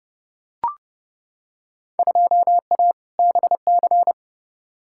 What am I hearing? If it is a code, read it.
Morse code, heard as a keyed tone, about 30 wpm, 700 Hz. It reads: 2A BC
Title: Field Day Exchanges - 30wpm